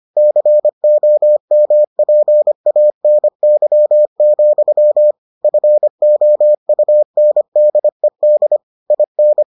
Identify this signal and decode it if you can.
Morse code, heard as a keyed tone, about 25 wpm, 600 Hz. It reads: COMPANY, FOUNDED IN